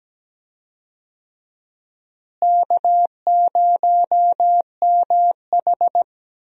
Morse code, heard as a keyed tone, about 17 wpm, 700 Hz. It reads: K0MH
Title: USA Call Signs - 4 Characters - Repeated 3x - 17wpm